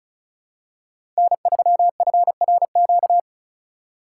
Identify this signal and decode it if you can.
Morse code, heard as a keyed tone, about 36 wpm, 700 Hz. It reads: N3FRQ
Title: USA Call Signs - 5 Characters - 4x Character Spacing - 35wpm